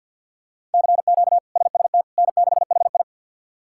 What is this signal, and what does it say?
Morse code, heard as a keyed tone, about 53 wpm, 700 Hz. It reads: CQ SST N6HI
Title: K1USN SST QSO Practice - 50wpm